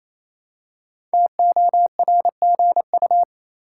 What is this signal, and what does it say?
Morse code, heard as a keyed tone, about 28 wpm, 700 Hz. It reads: TORGU
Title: Code Groups - 5 Letters - 28wpm